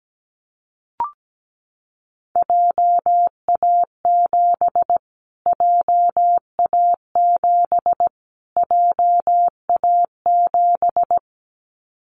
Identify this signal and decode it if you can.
Morse code, heard as a keyed tone, about 17 wpm, 700 Hz. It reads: JA7 JA7 JA7
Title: Callsign Prefixes for Top 25 DXCC - Repeated 3x - 17wpm